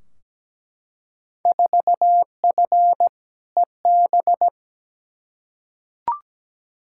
Morse code, heard as a keyed tone, about 17 wpm, 700 Hz. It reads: 4F EB
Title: Field Day Exchanges - 17wpm